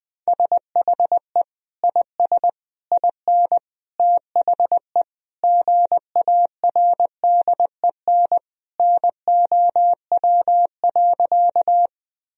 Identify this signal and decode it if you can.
Morse code, heard as a keyed tone, about 21 wpm, 700 Hz. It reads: SHE IS IN THE GARDEN NOW.
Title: Sentences from Top 1000 Words - Review Part 4 - 20wpm